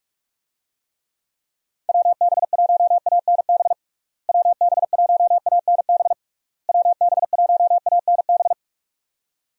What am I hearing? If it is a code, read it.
Morse code, heard as a keyed tone, about 46 wpm, 700 Hz. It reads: WB1ANB WB1ANB WB1ANB
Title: USA Call Signs - 6 Characters - Repeated 3x - 45wpm